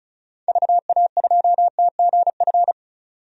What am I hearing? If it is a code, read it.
Morse code, heard as a keyed tone, about 36 wpm, 700 Hz. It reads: VA2TGF
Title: DXCC-100 Call Signs - 6 Characters - Repeated 3x - 35wpm